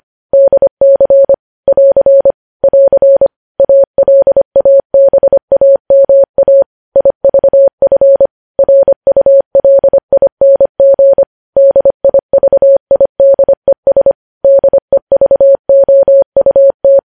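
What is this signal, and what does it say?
Morse code, heard as a keyed tone, about 25 wpm, 550 Hz. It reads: DC <AR> <AR> ALABAMA IVF RULING DIVIDES DEVOUT
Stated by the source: News headlines sent at 25.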